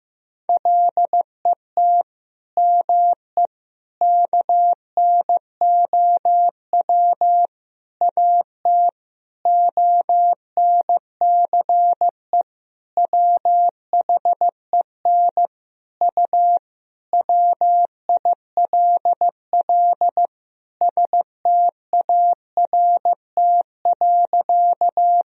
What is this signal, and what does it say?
Morse code, heard as a keyed tone, about 15 wpm, 700 Hz. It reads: LET ME KNOW AT ONCE WHEN U WILL START.
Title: Sentences from Top 300 Words and Abbreviations 15wpm